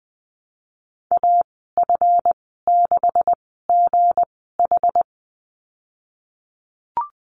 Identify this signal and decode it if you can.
Morse code, heard as a keyed tone, about 20 wpm, spaced as for 15 wpm, 700 Hz. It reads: AF6GH
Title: 167 Lesson 27g - Number 7 - Call Signs 20wpm